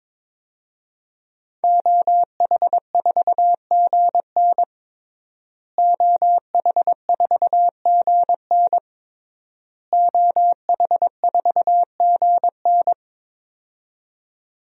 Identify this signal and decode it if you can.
Morse code, heard as a keyed tone, about 22 wpm, 700 Hz. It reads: OH4GN OH4GN OH4GN
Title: DXCC-100 Call Signs - 5 Characters - Repeated 3x - 22wpm